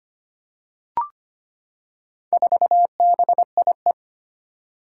Morse code, heard as a keyed tone, about 25 wpm, 700 Hz. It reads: 4BIE
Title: Four Challenging Letter-Number Combinations 25wpm